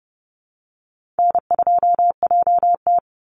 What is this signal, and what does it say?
Morse code, heard as a keyed tone, about 30 wpm, 700 Hz. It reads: N2JT
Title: Easy Call Signs Repeated 3x - 30wpm